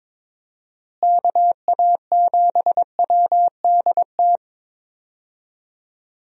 Morse code, heard as a keyed tone, about 22 wpm, 700 Hz. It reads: KA7WDT
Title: USA Call Signs - 6 Characters - Repeated 3x - 22wpm